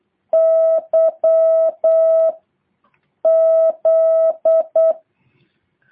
Morse code, heard as a keyed tone, about 8 wpm, 650 Hz. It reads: Y Z
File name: jh8cbh / 2023cw